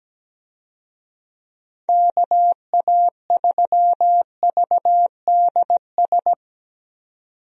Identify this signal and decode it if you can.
Morse code, heard as a keyed tone, about 17 wpm, 700 Hz. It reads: KA3VDS
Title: USA Call Signs - 6 Characters - 17wpm